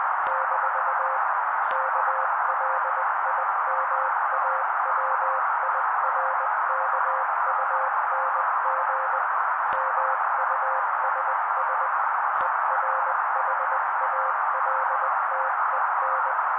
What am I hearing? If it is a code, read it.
Morse code, heard as a keyed tone, about 20 wpm, 530 Hz. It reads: - KLIMAWIRKUNG MUSS ERHALTEN
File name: daily / 18wpm noise